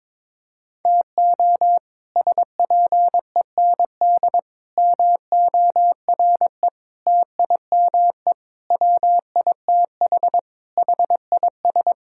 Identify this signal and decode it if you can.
Morse code, heard as a keyed tone, about 22 wpm, 700 Hz. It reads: TO SPEND MORE TIME WITH HIS
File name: practice / 22wpm / Sets of 7 Words - 22wpm